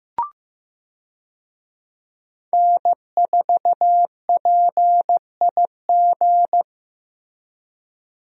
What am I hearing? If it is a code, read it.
Morse code, heard as a keyed tone, about 15 wpm, 700 Hz. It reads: N4PIG
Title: USA Call Signs - 5 Characters - Repeated 2x - 15wpm